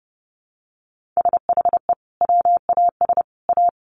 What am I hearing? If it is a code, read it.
Morse code, heard as a keyed tone, about 30 wpm, 700 Hz. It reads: SHE WAS A
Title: Sentences from Top 1100 Words 30wpm